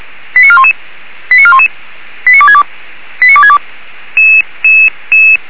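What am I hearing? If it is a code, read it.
Five-tone selective call (ZVEI), heard as quick runs of five tones, tones of about 70 ms. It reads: 89411 89411 89262 89262
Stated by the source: ZVEI five-tone selective call received on an SDR, recorded by Feuerwehr Orlamünde fire service.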